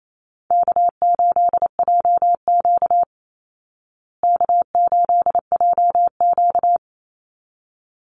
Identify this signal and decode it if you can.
Morse code, heard as a keyed tone, about 28 wpm, 700 Hz. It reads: K8JQ K8JQ
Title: Call Signs Repeated 3x - 28wpm